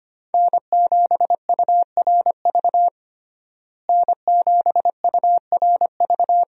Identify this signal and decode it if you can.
Morse code, heard as a keyed tone, about 25 wpm, 700 Hz. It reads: N7URV N7URV
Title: USA Call Signs - 5 Characters - Repeated 3x - 25wpm